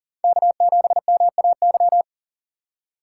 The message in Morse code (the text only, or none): K7MAY